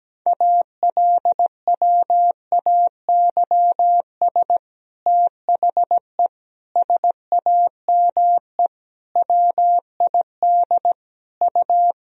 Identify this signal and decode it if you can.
Morse code, heard as a keyed tone, about 17 wpm, 700 Hz. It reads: ALWAYS THE SAME WID U
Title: Sentences from Top 300 Words and Abbreviations 17wpm